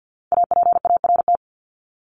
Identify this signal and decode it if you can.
Morse code, heard as a keyed tone, about 50 wpm, 700 Hz. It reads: APART
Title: Top 1501-1750 Words 50wpm